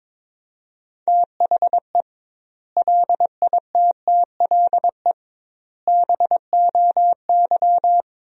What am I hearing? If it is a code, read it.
Morse code, heard as a keyed tone, about 22 wpm, 700 Hz. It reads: THE LITTLE BOY
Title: 2x Word-Spacing - Sentences from Top 500 Words - 22wpm